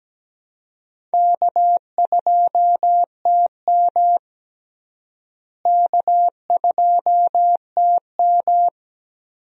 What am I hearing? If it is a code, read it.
Morse code, heard as a keyed tone, about 17 wpm, 700 Hz. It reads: K2TM K2TM
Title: USA Call Signs - 4 Characters - Repeated 2x - 17wpm